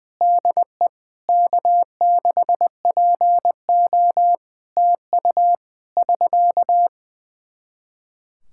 Morse code, heard as a keyed tone, about 20 wpm, 700 Hz. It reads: DE K6PO TU <SK>
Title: Practice QSO - Exchange 4 - Ending - 20wpm